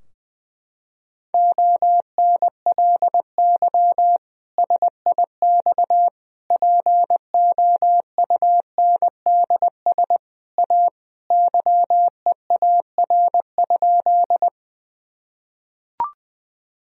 Morse code, heard as a keyed tone, about 21 wpm, 700 Hz. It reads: ONLY SIX POUNDS A YEAR?
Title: Sentences from Top 1000 Words - Review Part 2 - 20wpm